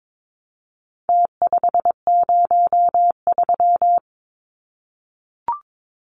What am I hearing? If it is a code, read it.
Morse code, heard as a keyed tone, about 22 wpm, 700 Hz. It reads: T503
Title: Numbers - Contest Style - 4-Digit 22wpm